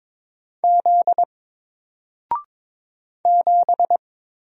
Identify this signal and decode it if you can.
Morse code, heard as a keyed tone, about 22 wpm, 700 Hz. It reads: Z 7
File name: Single-Letter-Number-22wpm